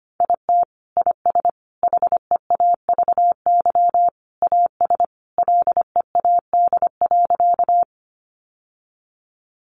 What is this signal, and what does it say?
Morse code, heard as a keyed tone, about 25 wpm, 700 Hz. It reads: IT IS HEAVY AS LEAD.
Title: Sentences from Top 1100 Words 25wpm